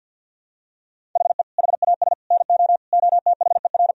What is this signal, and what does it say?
Morse code, heard as a keyed tone, about 50 wpm, 700 Hz. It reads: HE HAS NO OTHER
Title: Sentences from Top 1000 Words - Review Part 4 - 50wpm